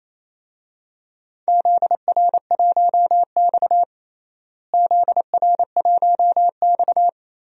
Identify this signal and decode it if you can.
Morse code, heard as a keyed tone, about 28 wpm, 700 Hz. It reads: ZR1X ZR1X
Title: DXCC-100 Call Signs - 4 Characters - Repeated 2x - 28wpm